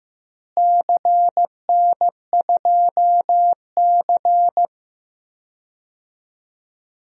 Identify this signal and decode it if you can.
Morse code, heard as a keyed tone, about 15 wpm, 700 Hz. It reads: CN2C
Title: DXCC-100 Call Signs - 4 Characters - Repeated 3x - 15wpm